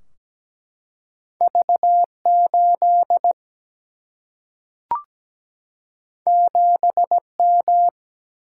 Morse code, heard as a keyed tone, about 17 wpm, 700 Hz. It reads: V 8 7 M
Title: Single-Letter Single-Number 17wpm